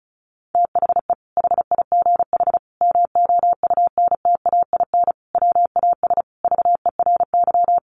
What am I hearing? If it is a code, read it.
Morse code, heard as a keyed tone, about 34 wpm, 700 Hz. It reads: THE HIGH MOUNTAIN WAS VERY
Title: Sentences from Top 1300 Words 35wpm